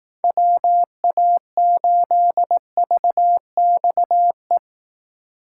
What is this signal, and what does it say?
Morse code, heard as a keyed tone, about 19 wpm, 700 Hz. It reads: WA8VXE
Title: Speed Racing - USA 2x3 Call Signs 15wpm